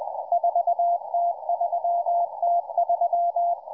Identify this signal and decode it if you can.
Morse code, heard as a keyed tone, about 20 wpm, 690 Hz, with radio noise. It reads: 4T3T3